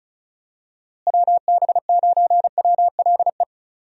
Morse code, heard as a keyed tone, about 34 wpm, 700 Hz. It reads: WB9WLE
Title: USA Call Signs - 6 Characters - Repeated 3x - 35wpm